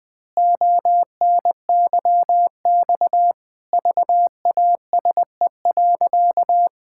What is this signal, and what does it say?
Morse code, heard as a keyed tone, about 20 wpm, 700 Hz. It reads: ONYX VASE.
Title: Pangram Challenge v2 - 20wpm